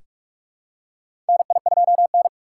Morse code, heard as a keyed tone, about 46 wpm, 700 Hz. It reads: NI2N
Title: USA Call Signs - 2x Character Spacing - 45wpm